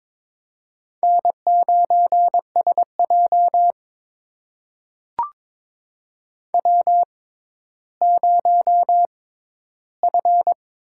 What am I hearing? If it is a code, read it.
Morse code, heard as a keyed tone, about 22 wpm, spaced as for 12 wpm, 700 Hz. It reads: N9SJ W0F
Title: USA Call Signs - 6x Character Spacing - 22wpm